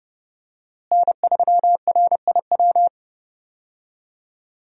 Morse code, heard as a keyed tone, about 28 wpm, 700 Hz. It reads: N3RIW
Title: USA Call Signs - 5 Characters - Repeated 3x - 30wpm